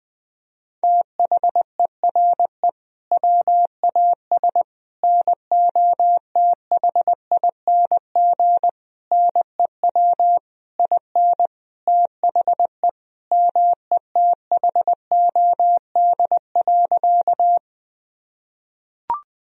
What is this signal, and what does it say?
Morse code, heard as a keyed tone, about 20 wpm, 700 Hz. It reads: THERE WAS NOTHING NEW IN THE METHOD.
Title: Sentences from Top 1000 Words - Review Part 3 - 20wpm